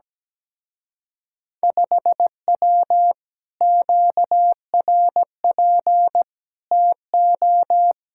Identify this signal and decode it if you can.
Morse code, heard as a keyed tone, about 17 wpm, 700 Hz. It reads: 5W QRP TO
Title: Speed-Racing Common QSO Phrases Encore - 17wpm